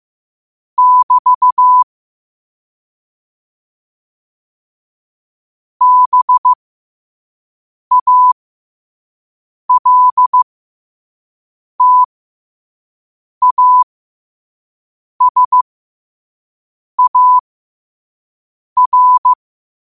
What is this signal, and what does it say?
Morse code, heard as a keyed tone, about 15 wpm, spaced as for 5 wpm, 1000 Hz. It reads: <BT> BALTASAR
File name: qotdfiles / 5wpm